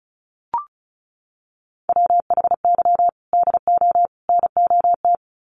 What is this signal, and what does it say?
Morse code, heard as a keyed tone, about 34 wpm, 700 Hz. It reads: WHY DO NOT
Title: Sets of 3 Words 35wpm